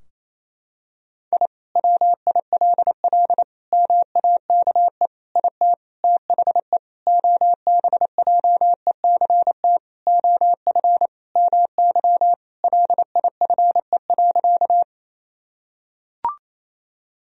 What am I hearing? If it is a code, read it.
Morse code, heard as a keyed tone, about 28 wpm, 700 Hz. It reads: I WILL MAKE IT THE OBJECT OF MY LIFE.
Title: Sentences from Top 1000 Words - Review Part 3 - 28wpm